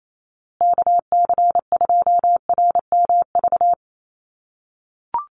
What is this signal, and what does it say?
Morse code, heard as a keyed tone, about 28 wpm, 700 Hz. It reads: KC2RMV
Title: USA Call Signs Repeated 3x 28wpm